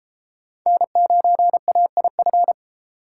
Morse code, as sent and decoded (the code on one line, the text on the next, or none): -. ----. .- .. ..-.
N9AIF